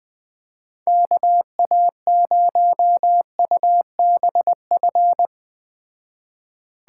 Morse code, heard as a keyed tone, about 20 wpm, 700 Hz. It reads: KA0UBF